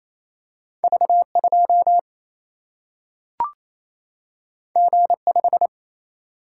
Morse code, heard as a keyed tone, about 28 wpm, 700 Hz. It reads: V2 G5
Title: Single-Letter Single-Number 28wpm